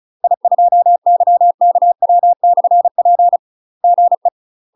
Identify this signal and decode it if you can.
Morse code, heard as a keyed tone, about 36 wpm, 700 Hz. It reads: I2YKW/P GE
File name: UK MASTER QSOs TRAINING / 35WPM / UK MASTER QSOs 0022